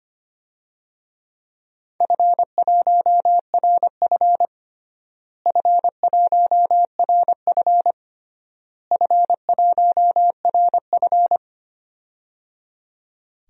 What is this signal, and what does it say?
Morse code, heard as a keyed tone, about 25 wpm, 700 Hz. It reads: F1RF F1RF F1RF
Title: DXCC-100 Call Signs - 4 Characters - Repeated 3x - 25wpm